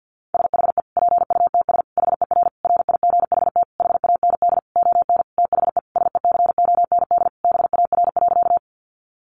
Morse code, heard as a keyed tone, about 50 wpm, 700 Hz. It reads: SHE PUTS HER RIGHT HAND ON THE SECOND BAR.
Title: Easy Phrases 50wpm